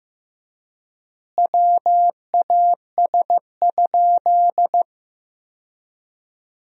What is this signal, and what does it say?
Morse code, heard as a keyed tone, about 15 wpm, 700 Hz. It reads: WAS?
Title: 3x Word-Spacing - Sentences from Top 200 Words - 15wpm